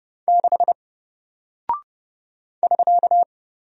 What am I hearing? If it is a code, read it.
Morse code, heard as a keyed tone, about 30 wpm, 700 Hz. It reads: 6 <SK>
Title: Single Character (Letter:Number:Punctuation:Prosign) Alt 30wpm